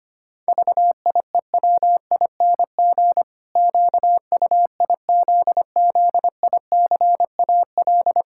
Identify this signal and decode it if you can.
Morse code, heard as a keyed tone, about 25 wpm, 700 Hz. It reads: VIEWING QUIZZICAL
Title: Pangram Challenge v2 - 25wpm